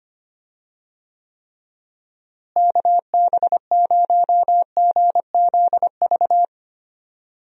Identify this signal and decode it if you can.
Morse code, heard as a keyed tone, about 25 wpm, 700 Hz. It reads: KB0GZV
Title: USA Call Signs - 6 Characters - Repeated 3x - 25wpm